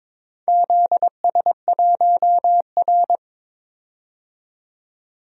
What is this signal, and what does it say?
Morse code, heard as a keyed tone, about 22 wpm, 700 Hz. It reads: ZS1R
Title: DXCC-100 Call Signs - 4 Characters - 22wpm